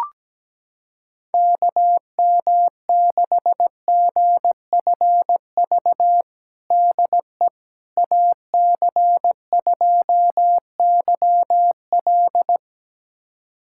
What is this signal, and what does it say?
Morse code, heard as a keyed tone, about 17 wpm, 700 Hz. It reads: KM6GFV DE AC2YL
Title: USA Callsign DE Callsign 17wpm